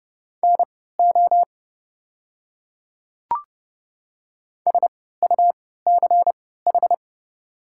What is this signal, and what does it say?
Morse code, heard as a keyed tone, about 30 wpm, spaced as for 17 wpm, 690 Hz. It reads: NO SUCH
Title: 3x Character Spacing - Top 200 Words - 30wpm